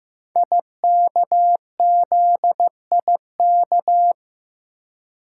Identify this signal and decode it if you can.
Morse code, heard as a keyed tone, about 16 wpm, 700 Hz. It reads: IKZIK
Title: Code Groups - 5 Letters - 15wpm